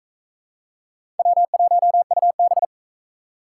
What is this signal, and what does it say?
Morse code, heard as a keyed tone, about 41 wpm, 700 Hz. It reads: W1UB